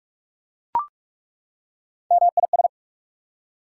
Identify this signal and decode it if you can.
Morse code, heard as a keyed tone, about 43 wpm, 690 Hz. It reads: MIS
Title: Prefixes and Suffixes 45wpm